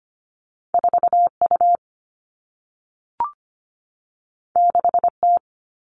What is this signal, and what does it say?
Morse code, heard as a keyed tone, about 25 wpm, 700 Hz. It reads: 4U 6T
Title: All Cut Numbers - 2-Digit 25wpm